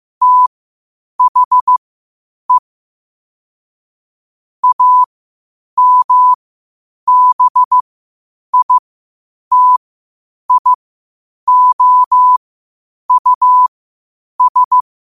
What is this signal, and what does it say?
Morse code, heard as a keyed tone, about 15 wpm, spaced as for 8 wpm, 1000 Hz. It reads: THE AMBITIOUS
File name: qotdfiles / 8wpm